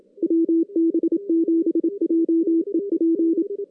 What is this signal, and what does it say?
Morse code, heard as a keyed tone, about 27 wpm, 330 Hz, with radio noise. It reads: WB7JEP
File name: 2016 JAN CW NAQP / 2016 Jan naqp k5ot - 0-6